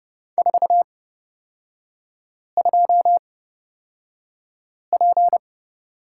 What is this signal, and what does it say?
Morse code, heard as a keyed tone, about 31 wpm, spaced as for 5 wpm, 700 Hz. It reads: 42P
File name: Lesson 38e - Letter q - Three Characters 30wpm